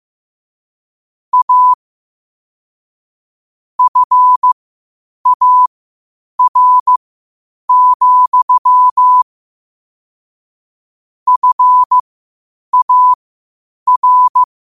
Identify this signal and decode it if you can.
Morse code, heard as a keyed tone, about 15 wpm, spaced as for 8 wpm, 1000 Hz. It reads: A FAR, FAR
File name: qotdfiles / 8wpm